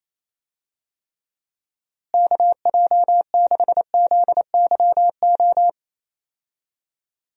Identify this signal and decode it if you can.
Morse code, heard as a keyed tone, about 27 wpm, 700 Hz. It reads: KJ6ZYO
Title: USA Call Signs - 6 Characters - Repeated 3x - 28wpm